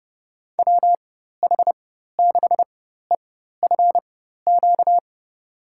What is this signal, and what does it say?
Morse code, heard as a keyed tone, about 30 wpm, spaced as for 14 wpm, 700 Hz. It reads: WH6EFQ
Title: USA Call Signs - 6 Characters - 4x Character Spacing - 30wpm